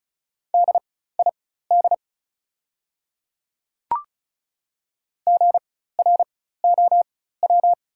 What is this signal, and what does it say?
Morse code, heard as a keyed tone, about 36 wpm, spaced as for 16 wpm, 700 Hz. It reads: DID GROW
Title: 4x Character Spacing - Top 300 Words - 35wpm